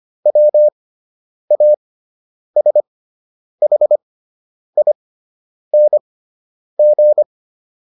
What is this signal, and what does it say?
Morse code, heard as a keyed tone, about 25 wpm, spaced as for 9 wpm, 600 Hz. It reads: WASHING